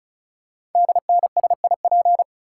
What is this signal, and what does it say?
Morse code, heard as a keyed tone, about 36 wpm, 700 Hz. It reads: DNSIP
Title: Code Groups - 5 Letters - 35wpm